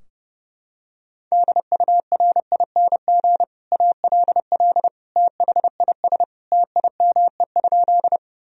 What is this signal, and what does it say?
Morse code, heard as a keyed tone, about 31 wpm, 700 Hz. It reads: DURING ALL THIS TIME?
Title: Sentences from Top 1000 Words - Review Part 2 - 30wpm